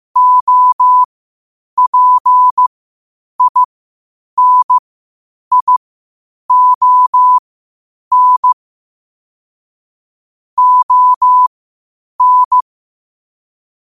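Morse code, heard as a keyed tone, about 15 wpm, spaced as for 8 wpm, 1000 Hz. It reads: OPINION ON
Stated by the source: Morse code quotes at 8 words per minute from QOTD at 8 WPM.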